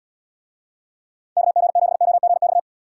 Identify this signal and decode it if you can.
Morse code, heard as a keyed tone, about 73 wpm, 700 Hz. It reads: KG7OGX